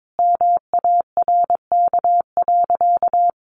Morse code, heard as a keyed tone, about 22 wpm, 700 Hz. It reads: MARK.
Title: Sentences from Top 1200 Words 22wpm